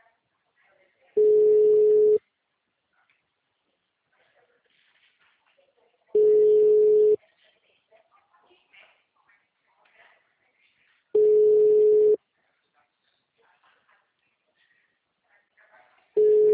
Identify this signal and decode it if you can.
Ringback tone (Europe, 425 Hz), with speech.
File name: out- -2011- - - .173